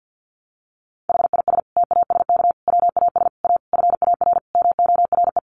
Morse code, heard as a keyed tone, about 50 wpm, 700 Hz. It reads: HIS TASK WAS A FAR MORE
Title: Sentences from Top 1200 Words 50wpm